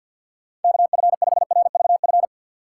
Morse code, heard as a keyed tone, about 51 wpm, 700 Hz. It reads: KF5RVF
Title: USA Call Signs - 3x Character Spacing - 50wpm